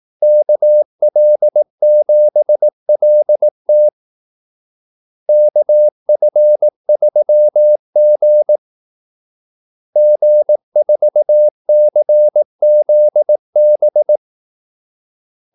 Morse code, heard as a keyed tone, about 18 wpm, 600 Hz. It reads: KL7LT KF3G G4CZB
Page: Kaldesignaler 18 wpm